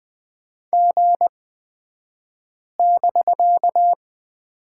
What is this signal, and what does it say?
Morse code, heard as a keyed tone, about 20 wpm, spaced as for 5 wpm, 700 Hz. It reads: G<BK>
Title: 249 Lesson 40d - BK - Two Characters 20wpm